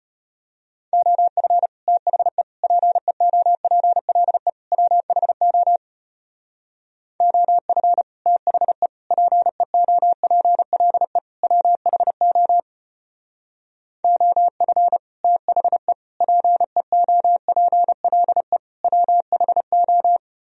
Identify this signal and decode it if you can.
Morse code, heard as a keyed tone, about 33 wpm, 700 Hz. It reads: OF THE PEOPLE WHO OF THE PEOPLE WHO OF THE PEOPLE WHO